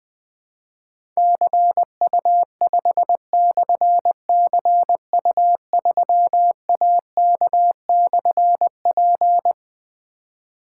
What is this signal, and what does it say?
Morse code, heard as a keyed tone, about 21 wpm, 700 Hz. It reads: CU5/CU3AK/P
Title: Difficult Call Signs 20wpm